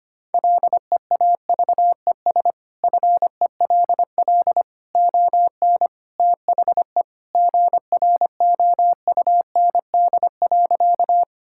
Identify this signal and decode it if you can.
Morse code, heard as a keyed tone, about 25 wpm, 700 Hz. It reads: LEAVES FELL ON THE GROUND.
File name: practice / 25wpm / Sentences with 3-Word Phrases - 25wpm